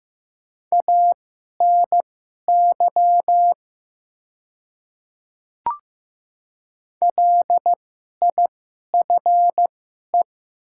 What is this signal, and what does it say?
Morse code, heard as a keyed tone, about 15 wpm, spaced as for 11 wpm, 700 Hz. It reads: ANY LIFE
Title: 2x Character Spacing - Top 500 Words - 15wpm